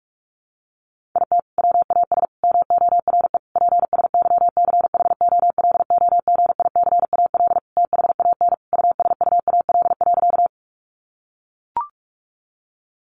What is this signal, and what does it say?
Morse code, heard as a keyed tone, about 44 wpm, 700 Hz. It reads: IT WAS MORE PSYCHOLOGICAL THAN USUAL.
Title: Sentences from Top 1400 Words 45wpm